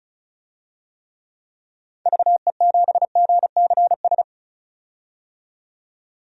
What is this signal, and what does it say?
Morse code, heard as a keyed tone, about 36 wpm, 700 Hz. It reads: VE7GCS
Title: DXCC-100 Call Signs - 6 Characters - 35wpm